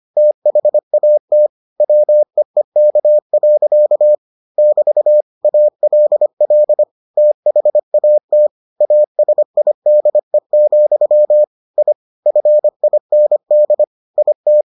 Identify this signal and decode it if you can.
Morse code, heard as a keyed tone, about 25 wpm, 600 Hz. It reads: THAT WEEK. <BT> ALL THAT ASIDE, I FIND IT